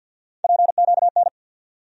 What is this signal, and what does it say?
Morse code, heard as a keyed tone, about 53 wpm, 700 Hz. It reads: PQN
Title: USA Call Sign Postfixes - 3 Letters 50wpm